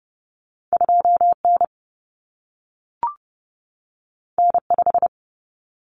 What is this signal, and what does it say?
Morse code, heard as a keyed tone, about 30 wpm, 700 Hz. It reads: 2N N5
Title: Common Cut Numbers - 2-Digit 30wpm